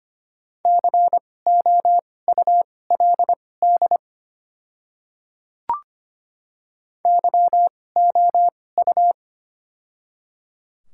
Morse code, heard as a keyed tone, about 25 wpm, spaced as for 18 wpm, 690 Hz. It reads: COULD YOU
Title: 2x Character Spacing - Top 100 Words - 25wpm